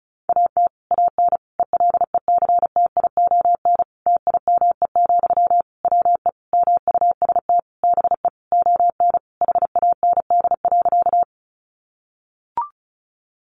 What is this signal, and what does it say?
Morse code, heard as a keyed tone, about 34 wpm, 700 Hz. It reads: AT AN ELECTION TIME, WE MUST BE ON HAND.
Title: Sentences from Top 1500 Words 35wpm